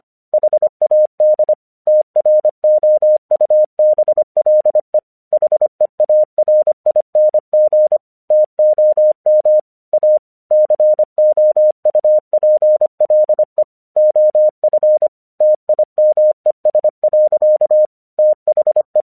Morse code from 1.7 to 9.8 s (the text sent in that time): TROUBLE HEARING TOM